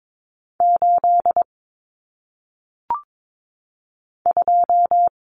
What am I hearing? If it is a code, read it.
Morse code, heard as a keyed tone, about 22 wpm, 700 Hz. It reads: 8 2
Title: Common Cut Numbers - 1-Digit 22wpm